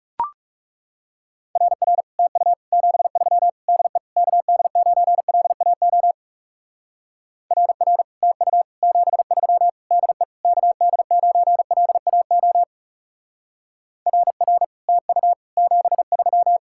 Morse code, heard as a keyed tone, about 36 wpm, 700 Hz. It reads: RR TU 73 DE KD9LAO RR TU 73 DE KD9LAO RR TU 73
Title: Speed-Racing Common QSO Phrases Encore - 30wpm